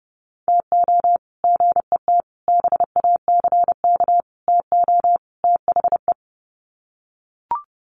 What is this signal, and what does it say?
Morse code, heard as a keyed tone, about 30 wpm, 700 Hz. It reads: TO GET BACK TO THE
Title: Sets of 5 Words Encore 30wpm